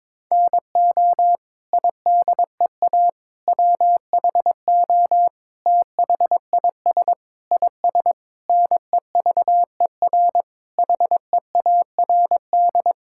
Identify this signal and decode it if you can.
Morse code, heard as a keyed tone, about 22 wpm, 700 Hz. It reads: NO IDEA WHO THIS IS NEVER HEARD
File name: Sets of 9 Words - 22wpm